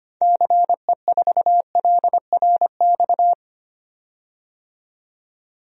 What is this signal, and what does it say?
Morse code, heard as a keyed tone, about 25 wpm, 700 Hz. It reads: CE4LRX
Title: DXCC-100 Call Signs - 6 Characters - 25wpm